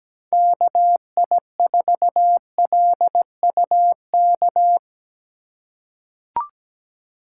Morse code, heard as a keyed tone, about 17 wpm, 700 Hz. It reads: KI4LUK
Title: USA Call Signs - 3x Character Spacing - 17wpm